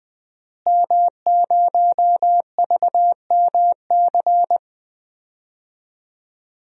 Morse code, heard as a keyed tone, about 20 wpm, 700 Hz. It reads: M0VMC
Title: DXCC-100 Call Signs - 5 Characters - Repeated 2x - 20wpm